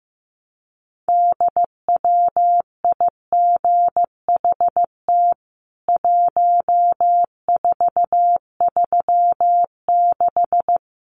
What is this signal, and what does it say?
Morse code, heard as a keyed tone, about 15 wpm, 700 Hz. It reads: DWIGHT 1436
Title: CWT Exchange 15wpm